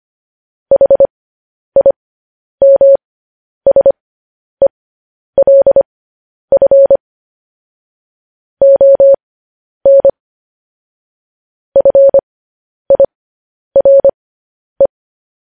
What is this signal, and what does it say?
Morse code, heard as a keyed tone, about 25 wpm, spaced as for 10 wpm, 550 Hz. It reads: HIMSELF ON FIRE
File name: podcast / News headlines-1x-25x10